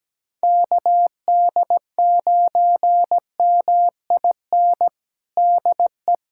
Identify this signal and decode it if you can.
Morse code, heard as a keyed tone, about 17 wpm, 700 Hz. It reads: KD9MIN DE
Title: USA Callsign DE Callsign 17wpm